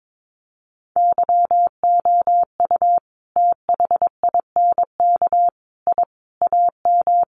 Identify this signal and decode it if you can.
Morse code, heard as a keyed tone, about 22 wpm, 700 Hz. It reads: YOU THINK I AM
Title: Sentences from Top 1200 Words 22wpm